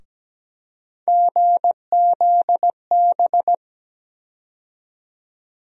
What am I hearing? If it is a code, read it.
Morse code, heard as a keyed tone, about 17 wpm, 700 Hz. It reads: GZB
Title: USA Call Sign Postfixes - 3 Letters 17wpm